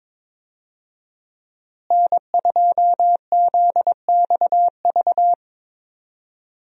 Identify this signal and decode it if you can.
Morse code, heard as a keyed tone, about 22 wpm, 700 Hz. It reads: N2ZXV
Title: USA Call Signs - 5 Characters - Repeated 3x - 22wpm